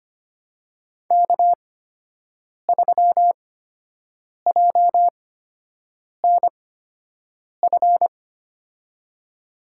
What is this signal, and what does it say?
Morse code, heard as a keyed tone, about 25 wpm, spaced as for 7 wpm, 700 Hz. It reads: K3JNF
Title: USA Call Signs - 5 Characters - 8x Character Spacing - 25wpm